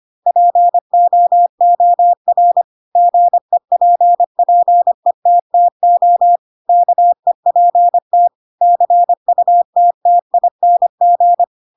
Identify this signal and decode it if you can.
Morse code, heard as a keyed tone, about 25 wpm, 700 Hz. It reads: POOR GEPPETTO KEPT CUTTING
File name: Pinocchio / Pinocchio ch3 25 25 ik1hge